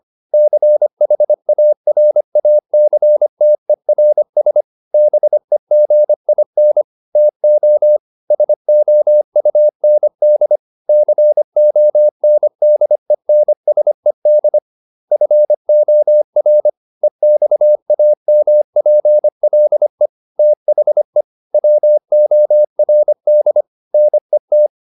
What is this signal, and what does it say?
Morse code, heard as a keyed tone, about 25 wpm, 600 Hz. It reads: CHARACTERS BEGIN TO SOUND CONDENSED FOR EXAMPLE THE WORD NET